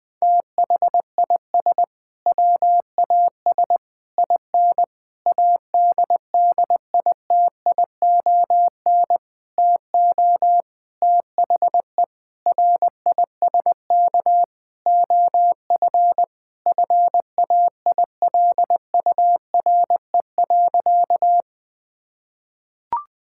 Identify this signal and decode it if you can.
Morse code, heard as a keyed tone, about 20 wpm, 700 Hz. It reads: THIS WAS IN ADDITION TO THE RISK OF FAILURE.
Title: Sentences from Top 1000 Words - Review Part 6 - 20wpm